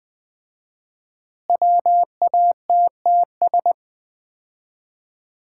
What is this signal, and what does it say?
Morse code, heard as a keyed tone, about 20 wpm, 700 Hz. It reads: WATTS
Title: Common QSO Phrases - 4x Word-Spacing - 20wpm